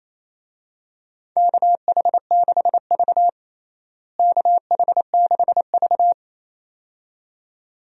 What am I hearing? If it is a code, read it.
Morse code, heard as a keyed tone, about 28 wpm, 700 Hz. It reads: KH6V KH6V
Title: DXCC-100 Call Signs - 4 Characters - Repeated 2x - 28wpm